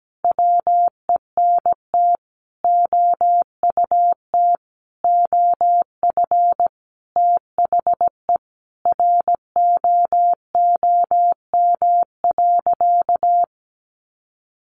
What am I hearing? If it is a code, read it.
Morse code, heard as a keyed tone, about 17 wpm, 700 Hz. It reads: WENT OUT OF THE ROOM.
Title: Sentences from Top 1100 Words 17wpm